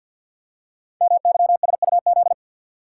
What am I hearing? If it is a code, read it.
Morse code, heard as a keyed tone, about 50 wpm, 700 Hz. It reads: MYSUZ
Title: Code Groups - 5 Letters - 50wpm